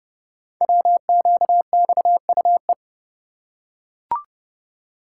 Morse code, heard as a keyed tone, about 30 wpm, 700 Hz. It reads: WQXUE
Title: Code Groups - 5 Letters - 30wpm